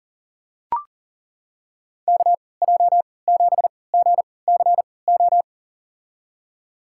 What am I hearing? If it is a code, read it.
Morse code, heard as a keyed tone, about 39 wpm, spaced as for 22 wpm, 700 Hz. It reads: KJ7GCO
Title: USA Call Signs - 3x Character Spacing - 40wpm